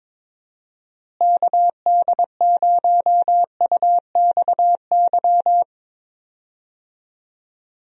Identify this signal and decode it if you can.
Morse code, heard as a keyed tone, about 22 wpm, 700 Hz. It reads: KD0UXY
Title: USA Call Signs - 6 Characters - Repeated 3x - 22wpm